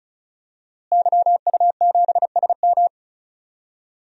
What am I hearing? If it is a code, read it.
Morse code, heard as a keyed tone, about 35 wpm, 700 Hz. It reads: YU7SM
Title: DXCC-100 Call Signs - 5 Characters - Repeated 3x - 35wpm